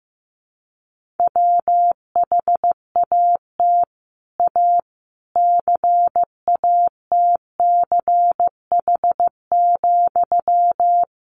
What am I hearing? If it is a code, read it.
Morse code, heard as a keyed tone, about 15 wpm, 700 Hz. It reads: WHAT A CATCH,
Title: Sentences from Top 1500 Words 15wpm